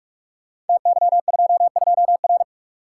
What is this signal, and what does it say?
Morse code, heard as a keyed tone, about 44 wpm, 700 Hz. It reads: TY22R
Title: Code Groups - 5 Letter:Numbers - 45wpm